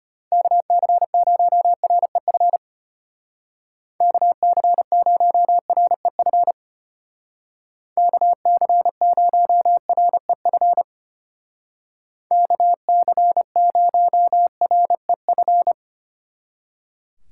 Morse code, heard as a keyed tone, about 31 wpm, 700 Hz. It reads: KC0REF KC0REF KC0REF KC0REF